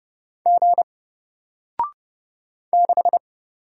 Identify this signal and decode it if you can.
Morse code, heard as a keyed tone, about 31 wpm, 700 Hz. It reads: G 6
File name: Single-Letter-Number-30wpm